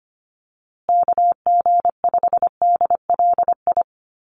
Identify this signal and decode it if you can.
Morse code, heard as a keyed tone, about 25 wpm, 700 Hz. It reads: KG5DLI
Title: USA Call Signs Repeated 3x 25wpm